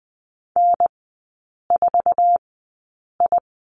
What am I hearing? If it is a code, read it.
Morse code, heard as a keyed tone, about 20 wpm, spaced as for 8 wpm, 700 Hz. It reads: N4I
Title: 047 Lesson 9e - Number 4 - Three Characters 20wpm